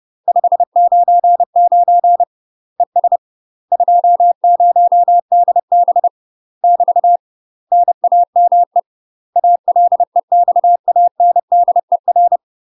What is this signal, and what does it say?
Morse code, heard as a keyed tone, about 30 wpm, 700 Hz. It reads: 599 ES 20DB <BT> NAME ALEXANDER
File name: UK MASTER QSOs TRAINING / 30WPM / UK MASTER QSOs 0001